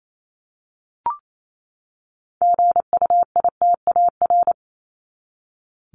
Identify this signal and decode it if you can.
Morse code, heard as a keyed tone, about 28 wpm, 700 Hz. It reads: GUITAR
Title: Top 1501-1750 Words 28wpm